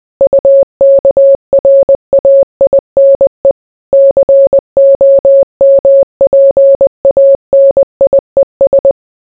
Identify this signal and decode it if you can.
Morse code, heard as a keyed tone, about 20 wpm, 550 Hz. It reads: UKRAINE COMPANIES